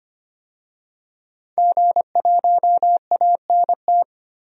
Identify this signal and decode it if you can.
Morse code, heard as a keyed tone, about 25 wpm, 700 Hz. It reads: G1ANT
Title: DXCC-100 Call Signs - 5 Characters - 25wpm